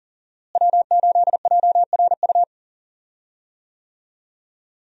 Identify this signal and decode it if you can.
Morse code, heard as a keyed tone, about 40 wpm, 700 Hz. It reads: W8JRU
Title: USA Call Signs - 5 Characters - Repeated 3x - 40wpm